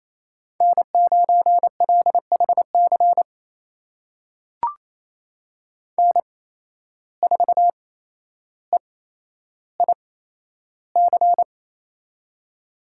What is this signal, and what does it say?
Morse code, heard as a keyed tone, about 28 wpm, spaced as for 12 wpm, 700 Hz. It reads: N9LHC N4EIC
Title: USA Call Signs - 5 Characters - 8x Character Spacing - 28wpm